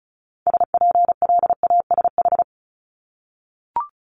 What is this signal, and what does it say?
Morse code, heard as a keyed tone, about 36 wpm, 700 Hz. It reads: SPLASH
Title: Multi-letter Phonemes 35wpm